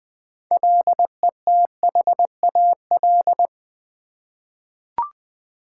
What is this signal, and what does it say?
Morse code, heard as a keyed tone, about 20 wpm, 700 Hz. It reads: LETHAL
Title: 6 Letter Words - First 3 Letters Given - 20wpm